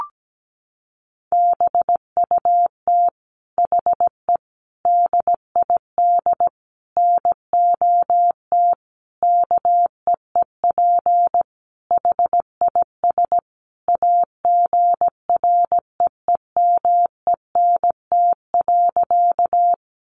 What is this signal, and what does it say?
Morse code, heard as a keyed tone, about 17 wpm, 700 Hz. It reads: BUT HE DID NOT KEEP HIS AGREEMENT.
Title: Sentences from Top 1500 Words 17wpm